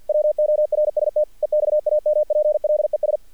Morse code, heard as a keyed tone, about 48 wpm, 600 Hz. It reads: WORST EXAMPLES
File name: archived / CRSnet